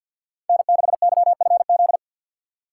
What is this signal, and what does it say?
Morse code, heard as a keyed tone, about 51 wpm, 690 Hz. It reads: N6YFZ
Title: USA Call Signs - 5 Characters - 3x Character Spacing - 50wpm